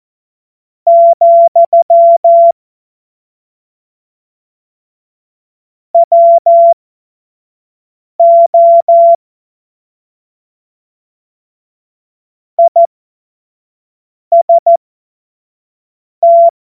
Morse code, heard as a keyed tone, about 14 wpm, spaced as for 5 wpm, 680 Hz. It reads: , WO IST
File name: Morsekurs / Witze / Witz 14. Februar 2023 25 F70